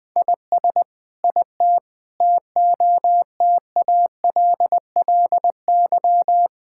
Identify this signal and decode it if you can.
Morse code, heard as a keyed tone, about 20 wpm, 700 Hz. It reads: IS IT TOTALLY
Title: Sentences from Top 1600 Words 20wpm